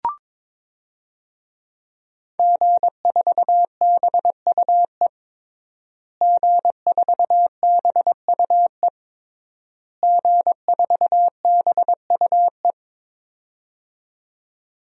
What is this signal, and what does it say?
Morse code, heard as a keyed tone, about 22 wpm, 700 Hz. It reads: G4BUE G4BUE G4BUE
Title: DXCC-100 Call Signs - 5 Characters - Repeated 3x - 22wpm